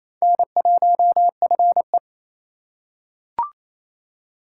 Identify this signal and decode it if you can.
Morse code, heard as a keyed tone, about 28 wpm, 700 Hz. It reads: N1FE
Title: K1USN SST QSO Practice - 28wpm